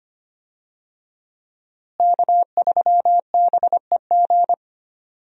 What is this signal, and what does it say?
Morse code, heard as a keyed tone, about 25 wpm, 700 Hz. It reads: K3BEG
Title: USA Call Signs - 5 Characters - Repeated 2x - 25wpm